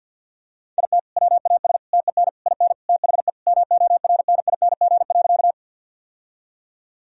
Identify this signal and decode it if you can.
Morse code, heard as a keyed tone, about 52 wpm, 700 Hz. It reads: IT WAS TEN IN THE MORNING.
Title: Sentences from Top 1000 Words - Review Part 2 - 50wpm